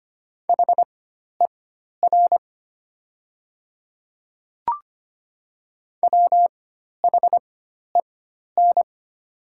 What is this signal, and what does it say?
Morse code, heard as a keyed tone, about 25 wpm, spaced as for 12 wpm, 700 Hz. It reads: HER WHEN
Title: 4x Character Spacing - Top 200 Words - 25wpm